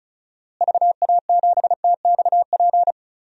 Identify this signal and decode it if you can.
Morse code, heard as a keyed tone, about 36 wpm, 700 Hz. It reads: VA7TXP
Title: DXCC-100 Call Signs - 6 Characters - Repeated 2x - 35wpm